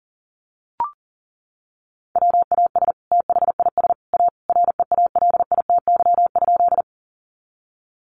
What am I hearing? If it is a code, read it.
Morse code, heard as a keyed tone, about 40 wpm, 700 Hz. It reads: WAS THIS A REALITY?
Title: Sentences from Top 1200 Words 40wpm